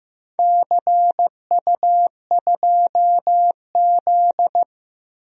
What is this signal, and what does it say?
Morse code, heard as a keyed tone, about 15 wpm, 700 Hz. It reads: CU2Z
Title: DXCC-100 Call Signs - 4 Characters - Repeated 2x - 15wpm